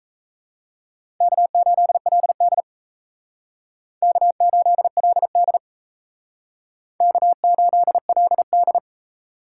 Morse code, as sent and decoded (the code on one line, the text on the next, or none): -.- ---.. .-.. -.. / -.- ---.. .-.. -.. / -.- ---.. .-.. -..
K8LD K8LD K8LD